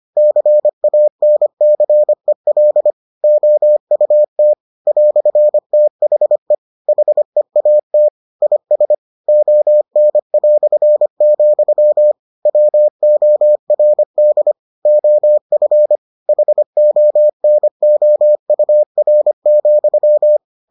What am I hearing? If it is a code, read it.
Morse code, heard as a keyed tone, about 25 wpm, 600 Hz. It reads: CANCEL OUT "THE HEAT IS ON", WORD OF HONOUR,